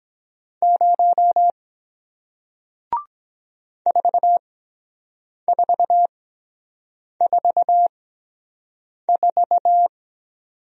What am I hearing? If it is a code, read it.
Morse code, heard as a keyed tone, about 23 wpm, spaced as for 7 wpm, 700 Hz. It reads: 0 4444